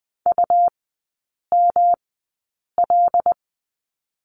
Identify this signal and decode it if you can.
Morse code, heard as a keyed tone, about 20 wpm, spaced as for 8 wpm, 700 Hz. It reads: UML
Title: 114 Lesson 19e - Letter w - Three Characters 20wpm